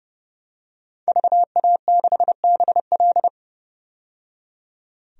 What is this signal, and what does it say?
Morse code, heard as a keyed tone, about 30 wpm, 700 Hz. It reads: VA6BL
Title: DXCC-100 Call Signs - 5 Characters - 30wpm